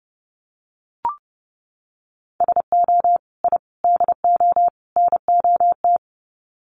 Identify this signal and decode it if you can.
Morse code, heard as a keyed tone, about 30 wpm, 700 Hz. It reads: SO I DO NOT
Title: Sets of 4 Words 30wpm